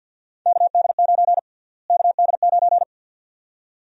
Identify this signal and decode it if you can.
Morse code, heard as a keyed tone, about 51 wpm, 700 Hz. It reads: KD9 KD9
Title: USA Call Sign Prefixes - Repeated 2x - 50wpm